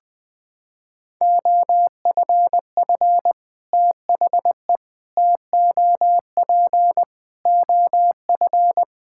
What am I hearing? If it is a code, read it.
Morse code, heard as a keyed tone, about 20 wpm, 700 Hz. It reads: OFF THE TOP OF
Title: Sets of 6 Words - 20wpm